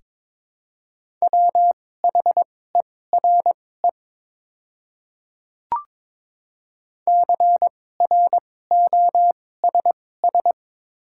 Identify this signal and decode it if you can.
Morse code, heard as a keyed tone, about 22 wpm, spaced as for 16 wpm, 700 Hz. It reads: WHERE CROSS
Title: 2x Character Spacing - Top 300 Words - 22wpm